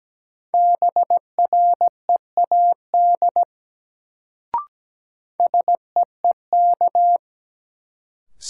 Morse code, heard as a keyed tone, about 17 wpm, 700 Hz. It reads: BREAD SEEK
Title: Sentences from Top 1600 Words - New Word Drill - 17wpm